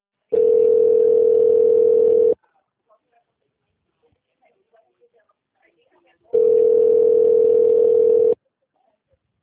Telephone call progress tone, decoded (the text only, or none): ringback tone